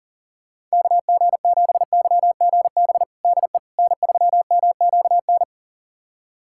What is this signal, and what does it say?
Morse code, heard as a keyed tone, about 40 wpm, 700 Hz. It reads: KG7YGB DE N3MQN
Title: USA Callsign DE Callsign 40wpm